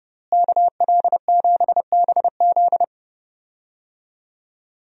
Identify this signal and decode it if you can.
Morse code, heard as a keyed tone, about 30 wpm, 700 Hz. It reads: KL7BZ
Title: DXCC-100 Call Signs - 5 Characters - Repeated 2x - 30wpm